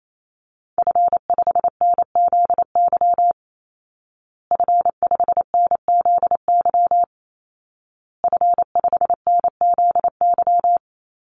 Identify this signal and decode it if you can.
Morse code, heard as a keyed tone, about 28 wpm, 700 Hz. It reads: F5NZY F5NZY F5NZY
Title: Call Signs Repeated 2x - 28wpm